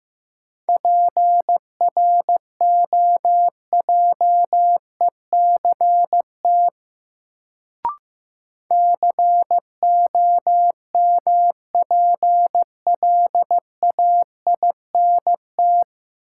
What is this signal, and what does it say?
Morse code, heard as a keyed tone, about 15 wpm, 700 Hz. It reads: PROJECT COMPLAINT
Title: Sentences from Top 1400 Words - New Word Drill Alt - 15wpm